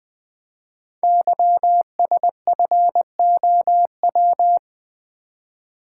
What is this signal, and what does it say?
Morse code, heard as a keyed tone, about 20 wpm, 700 Hz. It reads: YSFOW
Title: Code Groups - 5 Letter:Numbers - 20wpm